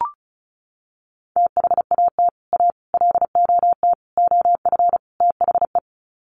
Morse code, heard as a keyed tone, about 34 wpm, 700 Hz. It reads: THAT A LOT OF THE
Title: Sets of 5 Words Encore2 35wpm